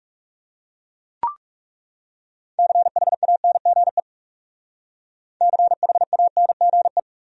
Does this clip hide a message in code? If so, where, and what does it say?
Yes: Morse code from 2.6 to 7.0 s, heard as a keyed tone, about 41 wpm, 700 Hz. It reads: CHANGE CHANGE